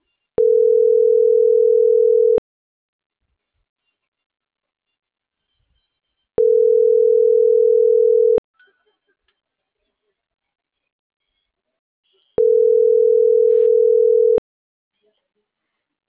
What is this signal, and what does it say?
Ringback tone (North America), 440 + 480 Hz.